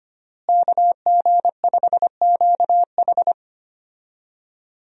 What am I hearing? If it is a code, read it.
Morse code, heard as a keyed tone, about 25 wpm, 700 Hz. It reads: KG5QH
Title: USA Call Signs - 5 Characters - Repeated 3x - 25wpm